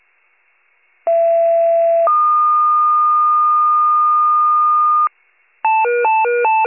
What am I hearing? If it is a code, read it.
Two-tone page, heard as a one-second tone, then a three-second tone, about 670 then 1220 Hz. It is heard from 1.1 to 5.1 s.
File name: pager / 83 RESCUE OFC / wcalert